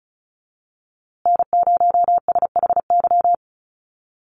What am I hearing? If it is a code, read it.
Morse code, heard as a keyed tone, about 34 wpm, 700 Hz. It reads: N0SHY
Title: USA Call Signs Repeated 2x 35wpm